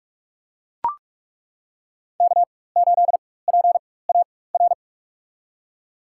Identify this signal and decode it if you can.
Morse code, heard as a keyed tone, about 42 wpm, spaced as for 21 wpm, 700 Hz. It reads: K8PAR
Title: USA Call Signs - 5 Characters - 4x Character Spacing - 45wpm